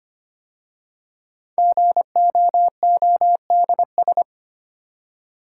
Morse code, heard as a keyed tone, about 25 wpm, 700 Hz. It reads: GOODS
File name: 5 Letter Words - First 3 Letters Given - 25wpm